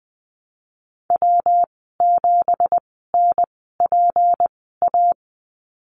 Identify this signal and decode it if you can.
Morse code, heard as a keyed tone, about 19 wpm, spaced as for 14 wpm, 700 Hz. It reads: W7NPA
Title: 245 Lesson 39g - Letter z - Call Signs 20wpm